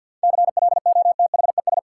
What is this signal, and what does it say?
Morse code, heard as a keyed tone, about 53 wpm, 700 Hz. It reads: CLOTHES
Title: Seven Letter Words - 50wpm